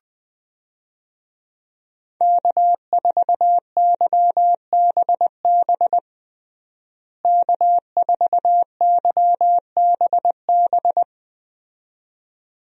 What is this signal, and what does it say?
Morse code, heard as a keyed tone, about 20 wpm, 700 Hz. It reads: K4YBB K4YBB
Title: USA Call Signs - 5 Characters - Repeated 2x - 20wpm